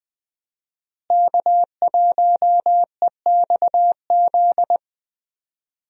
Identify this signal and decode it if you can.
Morse code, heard as a keyed tone, about 20 wpm, 700 Hz. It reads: K1EXZ
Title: USA Call Signs - 5 Characters - Repeated 3x - 20wpm